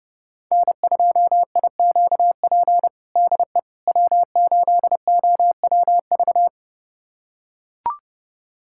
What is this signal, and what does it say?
Morse code, heard as a keyed tone, about 30 wpm, 700 Hz. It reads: N2IQP DE W8OWV
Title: USA Callsign DE Callsign - 5 Characters - 30wpm